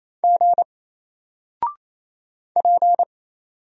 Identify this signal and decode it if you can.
Morse code, heard as a keyed tone, about 27 wpm, 700 Hz. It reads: G P
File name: Single-Letter-Number-28wpm